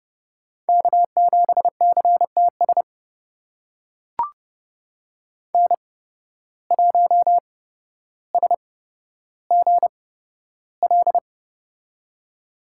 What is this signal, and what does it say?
Morse code, heard as a keyed tone, about 30 wpm, spaced as for 13 wpm, 700 Hz. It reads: K7CTS N1SGL
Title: USA Call Signs - 5 Characters - 8x Character Spacing - 30wpm